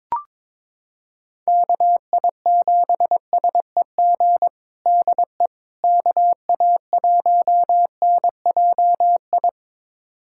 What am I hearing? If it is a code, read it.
Morse code, heard as a keyed tone, about 22 wpm, 700 Hz. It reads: KI7SEG DE KA1NJI
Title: USA Callsign DE Callsign - 6 Characters - 22wpm